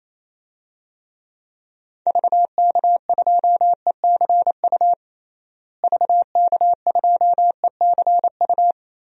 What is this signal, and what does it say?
Morse code, heard as a keyed tone, about 28 wpm, 700 Hz. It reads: VK2ECU VK2ECU
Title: DXCC-100 Call Signs - 6 Characters - Repeated 3x - 28wpm